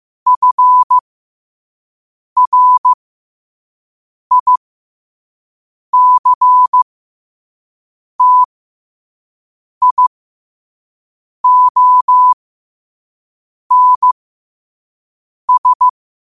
Morse code, heard as a keyed tone, about 15 wpm, spaced as for 5 wpm, 1000 Hz. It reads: FRICTIONS
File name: qotdfiles / 5wpm